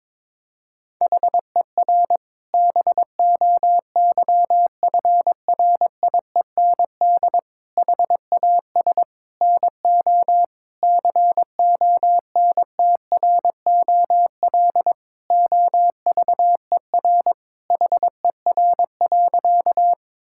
Morse code, heard as a keyed tone, about 22 wpm, 700 Hz. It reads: HER BOYFRIEND HAS NO CONTROL OVER HER.
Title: Sentences from Top 1000 Words - Review Part 5 - 22wpm